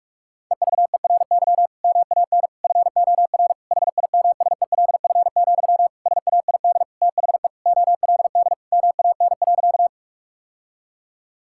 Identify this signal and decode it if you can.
Morse code, heard as a keyed tone, about 42 wpm, 700 Hz. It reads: EVERY MAN FOR HIMSELF, SAID THE OLD MAN.
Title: Sentences from Top 1600 Words 45wpm